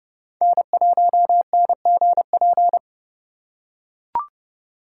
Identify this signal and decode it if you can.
Morse code, heard as a keyed tone, about 30 wpm, 700 Hz. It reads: N1NGP
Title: USA Call Signs - 5 Characters - 4x Character Spacing - 30wpm